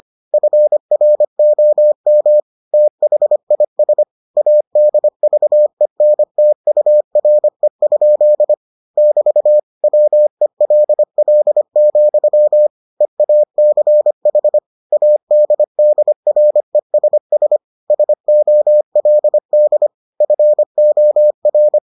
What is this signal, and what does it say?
Morse code, heard as a keyed tone, about 25 wpm, 600 Hz. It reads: FROM THIS ADVENTURE? <BT> WELL, EACH ADDRESS SOLD FOR